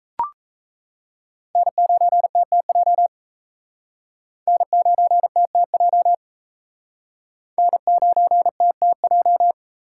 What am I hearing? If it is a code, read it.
Morse code, heard as a keyed tone, about 35 wpm, 700 Hz. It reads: N9TTJ N9TTJ N9TTJ
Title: Speed Racing - USA Call Signs 28wpm